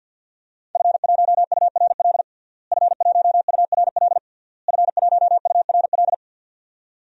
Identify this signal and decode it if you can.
Morse code, heard as a keyed tone, about 46 wpm, 700 Hz. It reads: F1URL F1URL F1URL
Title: DXCC-100 Call Signs - 5 Characters - Repeated 3x - 50wpm